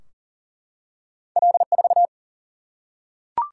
Morse code, heard as a keyed tone, about 38 wpm, 700 Hz. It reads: L4
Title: Single-Letter Single-Number 40wpm